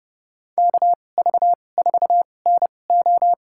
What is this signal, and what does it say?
Morse code, heard as a keyed tone, about 32 wpm, spaced as for 22 wpm, 700 Hz. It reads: KV4NO
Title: USA Call Signs - 5 Characters - 2x Character Spacing - 30wpm